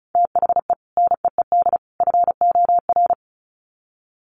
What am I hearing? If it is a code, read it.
Morse code, heard as a keyed tone, about 34 wpm, 700 Hz. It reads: THE NEED FOR
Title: Sets of 3 Words 35wpm